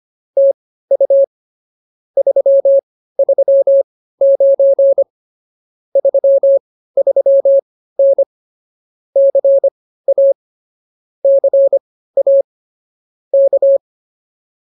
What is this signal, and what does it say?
Morse code, heard as a keyed tone, about 26 wpm, spaced as for 15 wpm, 550 Hz. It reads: TU 339 33N CA CA K